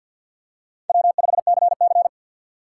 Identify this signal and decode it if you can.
Morse code, heard as a keyed tone, about 50 wpm, 700 Hz. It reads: W5CC
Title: USA Call Signs - 4 Characters - 3x Character Spacing - 50wpm